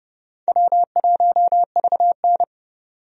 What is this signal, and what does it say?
Morse code, heard as a keyed tone, about 31 wpm, 700 Hz. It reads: W1VN
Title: USA Call Signs - 4 Characters - 30wpm